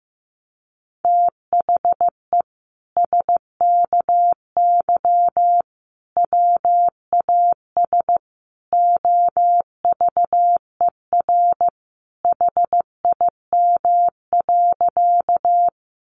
Morse code, heard as a keyed tone, about 15 wpm, 700 Hz. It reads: THE SKY WAS OVER HIM.
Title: Sentences from Top 1200 Words 15wpm